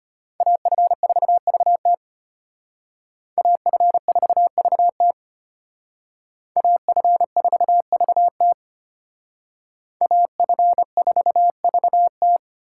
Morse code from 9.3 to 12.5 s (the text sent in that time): AF4VT